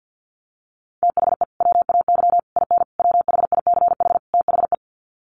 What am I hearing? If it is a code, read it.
Morse code, heard as a keyed tone, about 48 wpm, 700 Hz. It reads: THE WAY IN WHICH THE
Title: Sets of 5 Words Encore2 50wpm